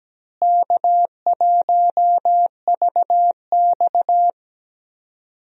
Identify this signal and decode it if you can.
Morse code, heard as a keyed tone, about 17 wpm, 700 Hz. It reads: K1VX
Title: USA Call Signs - 4 Characters - Repeated 3x - 17wpm